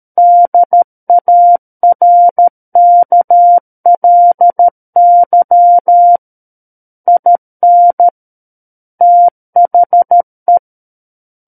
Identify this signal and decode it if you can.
Morse code, heard as a keyed tone, about 13 wpm, 700 Hz. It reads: DARKLY IN THE
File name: Book One / 04-Chapter 4 - 13 WPM